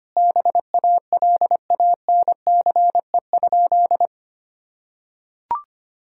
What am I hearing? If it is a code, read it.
Morse code, heard as a keyed tone, about 25 wpm, 700 Hz. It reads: BALANCE?
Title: Sentences from Top 1000 Words - Review Part 6 - 25wpm